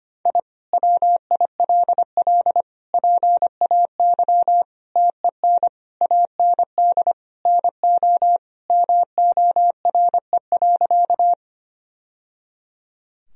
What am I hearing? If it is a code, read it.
Morse code, heard as a keyed tone, about 25 wpm, 700 Hz. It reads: I WILL PAY TEN AND NO MORE.
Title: Sentences from Top 1000 Words - Review Part 2 - 25wpm